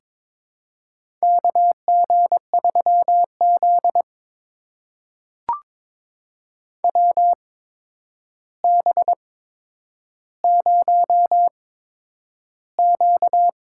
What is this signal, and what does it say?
Morse code, heard as a keyed tone, about 22 wpm, spaced as for 10 wpm, 700 Hz. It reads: KG3Z WB0Q
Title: USA Call Signs - 4 Characters - 8x Character Spacing - 22wpm